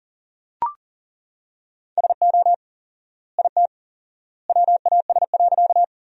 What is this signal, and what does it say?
Morse code, heard as a keyed tone, about 40 wpm, 700 Hz. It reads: SO IT WAS.
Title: 4x Word-Spacing - Sentences from Top 100 Words - 40wpm